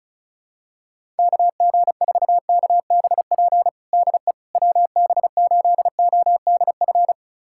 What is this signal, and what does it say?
Morse code, heard as a keyed tone, about 34 wpm, 700 Hz. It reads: KG4KBP DE WB8ODF
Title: USA Callsign DE Callsign 35wpm